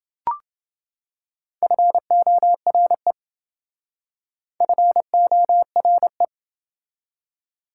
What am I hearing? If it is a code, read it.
Morse code, heard as a keyed tone, about 28 wpm, 700 Hz. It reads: FORE FORE